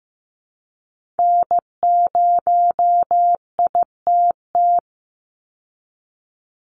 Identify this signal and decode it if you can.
Morse code, heard as a keyed tone, about 15 wpm, 700 Hz. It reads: N0ITT
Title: USA Call Signs 15wpm